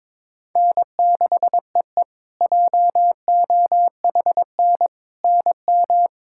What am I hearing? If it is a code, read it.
Morse code, heard as a keyed tone, about 22 wpm, 700 Hz. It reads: N6EE JOHN NM
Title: K1USN SST QSO Practice - 22wpm